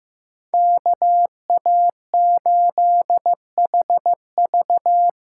Morse code, heard as a keyed tone, about 15 wpm, 700 Hz. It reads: KA8HV
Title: USA Call Signs - 8x Character Spacing - 15wpm